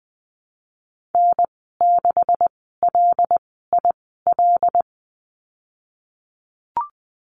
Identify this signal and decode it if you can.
Morse code, heard as a keyed tone, about 20 wpm, spaced as for 15 wpm, 700 Hz. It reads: N6LIL
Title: 193 Lesson 31g - Letter v - Call Signs 20wpm